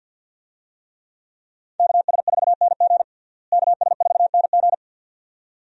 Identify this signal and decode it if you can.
Morse code, heard as a keyed tone, about 50 wpm, 700 Hz. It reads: KS4NG KS4NG
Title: USA Call Signs - 5 Characters - Repeated 2x - 50wpm